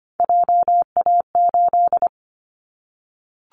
Morse code, heard as a keyed tone, about 25 wpm, 700 Hz. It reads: JA8
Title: Callsign Prefixes for Top 25 DXCC - Repeated 3x - 25wpm